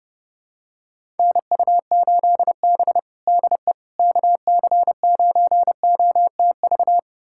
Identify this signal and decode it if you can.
Morse code, heard as a keyed tone, about 30 wpm, 700 Hz. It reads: NU8B DE KC9OTV
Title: USA Callsign DE Callsign 30wpm